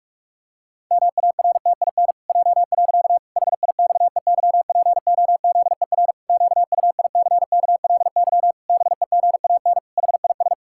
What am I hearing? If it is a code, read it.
Morse code, heard as a keyed tone, about 41 wpm, 700 Hz. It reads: MARTIN J. HIXEYPOZER QUICKLY BEGAN HIS
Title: Pangram Challenge v2 - 45wpm